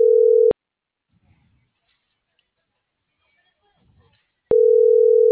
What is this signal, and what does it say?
Ringback tone (North America), 440 + 480 Hz.